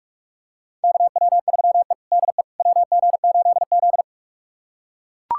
Morse code, heard as a keyed tone, about 41 wpm, 700 Hz. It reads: KW3E DE WG8Z
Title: USA Callsign DE Callsign - 4 Characters - 45wpm